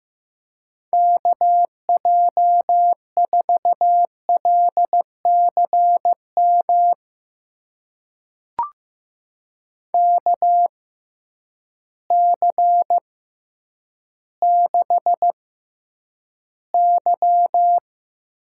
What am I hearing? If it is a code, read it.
Morse code, heard as a keyed tone, about 15 wpm, spaced as for 9 wpm, 700 Hz. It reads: KJ4LCM KC6Y
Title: USA Call Signs - 6x Character Spacing - 15wpm